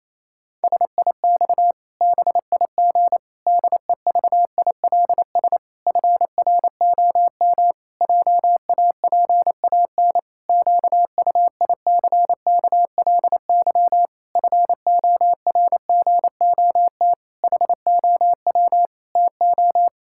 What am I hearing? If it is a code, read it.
Morse code, heard as a keyed tone, about 28 wpm, 700 Hz. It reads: SIX BIG DEVILS FROM JAPAN QUICKLY FORGOT HOW TO
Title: Pangram Challenge v2 - 28wpm